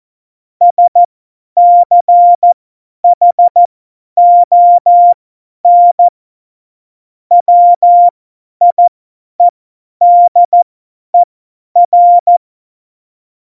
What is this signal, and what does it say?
Morse code, heard as a keyed tone, about 14 wpm, spaced as for 10 wpm, 700 Hz. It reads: SCHON WIEDER
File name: Morsekurs / Witze / Witz 07 Juli 2020, 50 F70